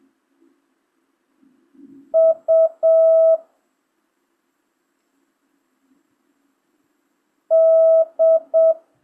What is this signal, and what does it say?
Morse code, heard as a keyed tone, about 7 wpm, 1300 Hz. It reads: U D
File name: jh8cbh / 2025cw